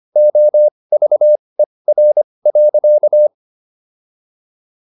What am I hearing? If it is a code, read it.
Morse code, heard as a keyed tone, about 25 wpm, spaced as for 20 wpm, 600 Hz. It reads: OVER.